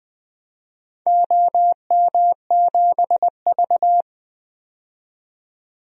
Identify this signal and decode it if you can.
Morse code, heard as a keyed tone, about 20 wpm, 700 Hz. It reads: OM7V
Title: DXCC-100 Call Signs - 4 Characters - 20wpm